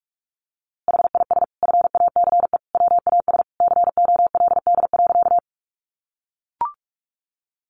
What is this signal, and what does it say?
Morse code, heard as a keyed tone, about 44 wpm, 700 Hz. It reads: HIS FACE WAS COLD.
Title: A Sound Of Thunder 45wpm